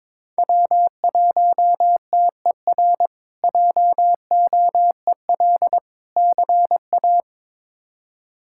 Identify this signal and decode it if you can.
Morse code, heard as a keyed tone, about 21 wpm, 700 Hz. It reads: W1TER JOEL CA
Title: K1USN SST QSO Practice - 22wpm